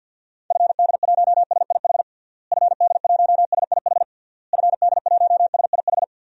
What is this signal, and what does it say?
Morse code, heard as a keyed tone, about 47 wpm, 700 Hz. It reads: FD1SIH FD1SIH FD1SIH
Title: DXCC-100 Call Signs - 6 Characters - Repeated 3x - 50wpm